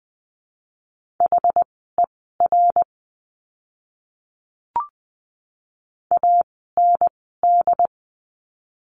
Morse code, heard as a keyed tone, about 20 wpm, spaced as for 14 wpm, 700 Hz. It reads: HER AND
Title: 230 Lesson 37f - Letter x - Words 20wpm